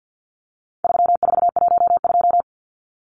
Morse code, heard as a keyed tone, about 50 wpm, 700 Hz. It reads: 3412
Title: Numbers - 4-Digit 50wpm